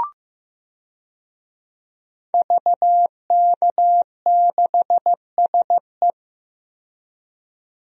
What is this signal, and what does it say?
Morse code, heard as a keyed tone, about 15 wpm, 700 Hz. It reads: VK6SE
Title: DXCC-100 Call Signs - 5 Characters - 15wpm